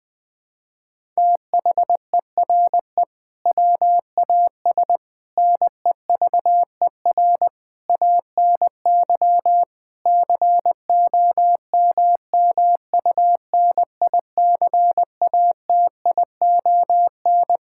Morse code, heard as a keyed tone, about 20 wpm, 700 Hz. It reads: THERE WAS NEVER ANY COMMUNICATION
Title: Sentences from Top 1000 Words - Review Part 6 - 20wpm